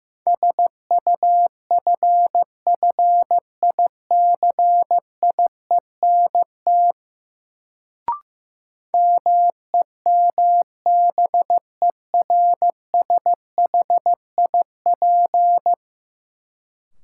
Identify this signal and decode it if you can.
Morse code, heard as a keyed tone, about 15 wpm, 700 Hz. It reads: SUFFICIENT MEMBERSHIP
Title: Sentences from Top 1400 Words - New Word Drill Alt - 15wpm